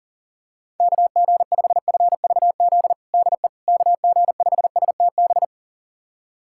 Morse code, heard as a keyed tone, about 37 wpm, 700 Hz. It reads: KG5FVZ DE KG5STB
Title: USA Callsign DE Callsign - 6 Characters - 40wpm